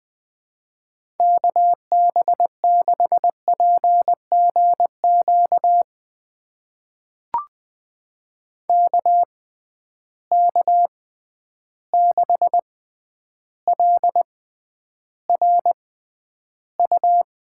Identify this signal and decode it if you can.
Morse code, heard as a keyed tone, about 21 wpm, spaced as for 10 wpm, 700 Hz. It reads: KB6PGQ KK6LRU
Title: USA Call Signs - 6 Characters - 6x Character Spacing - 20wpm